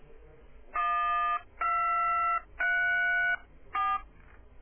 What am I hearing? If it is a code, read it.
DTMF, heard as telephone touch tones, tones of about 690 ms. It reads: 1234